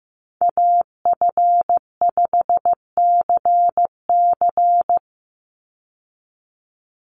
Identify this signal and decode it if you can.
Morse code, heard as a keyed tone, about 15 wpm, 700 Hz. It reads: AF5CC
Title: Call Signs Repeated 3x - 15wpm